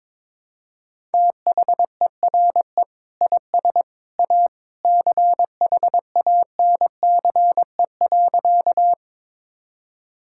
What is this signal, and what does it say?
Morse code, heard as a keyed tone, about 22 wpm, 700 Hz. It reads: THERE IS A CHANCE.
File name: practice / 22wpm / Sentences with 2-Word Phrases - 22wpm